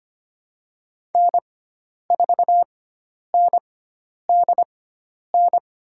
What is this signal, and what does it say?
Morse code, heard as a keyed tone, about 25 wpm, spaced as for 10 wpm, 700 Hz. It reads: N4NDN
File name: Lesson 16g - Letter c - Call Signs 25wpm